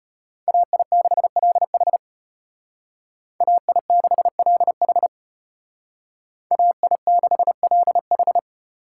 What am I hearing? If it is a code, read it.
Morse code, heard as a keyed tone, about 31 wpm, 700 Hz. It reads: AI6LH AI6LH AI6LH
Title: Speed Racing - USA 2x2 Call Signs 25wpm